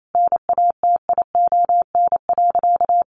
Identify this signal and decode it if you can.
Morse code, heard as a keyed tone, about 28 wpm, 700 Hz. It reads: NATION.
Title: Sentences from Top 1200 Words 28wpm